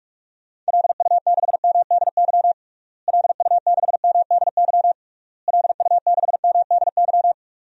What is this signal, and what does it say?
Morse code, heard as a keyed tone, about 42 wpm, 700 Hz. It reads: LU6MDY LU6MDY LU6MDY
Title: DXCC-100 Call Signs - 6 Characters - Repeated 3x - 45wpm